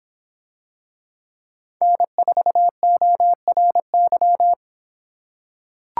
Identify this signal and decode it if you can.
Morse code, heard as a keyed tone, about 25 wpm, 700 Hz. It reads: N4ORY